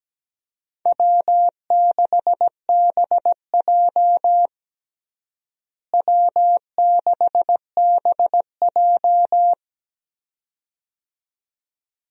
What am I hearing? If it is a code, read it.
Morse code, heard as a keyed tone, about 17 wpm, 700 Hz. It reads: W6BJ W6BJ
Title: USA Call Signs - 4 Characters - Repeated 2x - 17wpm